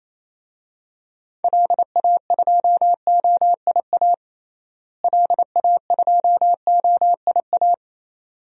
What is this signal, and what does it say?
Morse code, heard as a keyed tone, about 28 wpm, 700 Hz. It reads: LA2OIA LA2OIA
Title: DXCC-100 Call Signs - 6 Characters - Repeated 2x - 28wpm